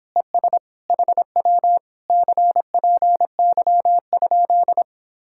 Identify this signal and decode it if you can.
Morse code, heard as a keyed tone, about 26 wpm, 700 Hz. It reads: ES HW CPY?
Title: Speed-Racing Common QSO Phrases Encore - 22wpm